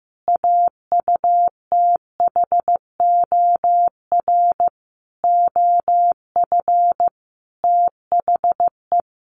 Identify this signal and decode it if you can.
Morse code, heard as a keyed tone, about 15 wpm, 700 Hz. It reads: AUTHOR OF THE
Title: Sentences from Top 1500 Words 15wpm